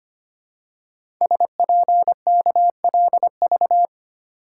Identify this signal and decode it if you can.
Morse code, heard as a keyed tone, about 25 wpm, 700 Hz. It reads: SPKLV
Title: Code Groups - 5 Letters - 25wpm